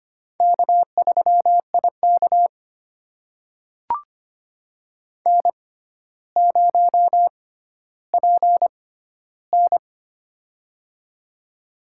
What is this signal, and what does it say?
Morse code, heard as a keyed tone, about 25 wpm, spaced as for 13 wpm, 700 Hz. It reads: K3IK N0PN
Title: USA Call Signs - 4 Characters - 6x Character Spacing - 25wpm